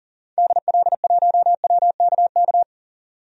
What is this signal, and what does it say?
Morse code, heard as a keyed tone, about 40 wpm, 700 Hz. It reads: DL1WKK
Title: DXCC-100 Call Signs - 6 Characters - 40wpm